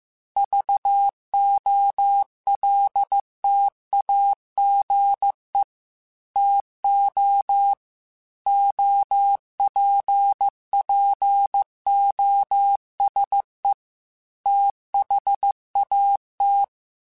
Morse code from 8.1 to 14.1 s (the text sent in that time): OPPOSE